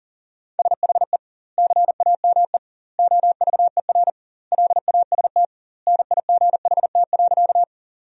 Morse code, heard as a keyed tone, about 40 wpm, 700 Hz. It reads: SHE CAME OVER LAST NIGHT.
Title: 2x Word-Spacing - Sentences from Top 300 Words - 40wpm